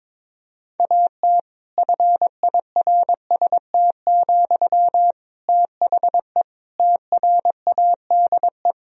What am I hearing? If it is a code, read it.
Morse code, heard as a keyed tone, about 21 wpm, 700 Hz. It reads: AT FIRST, THE TRADE
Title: Sentences from Top 1000 Words - Review Part 4 - 22wpm